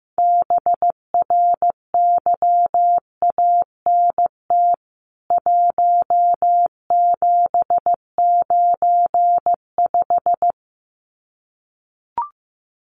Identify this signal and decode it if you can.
Morse code, heard as a keyed tone, about 15 wpm, 700 Hz. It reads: BRYANT 1795
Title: CWT Exchange 15wpm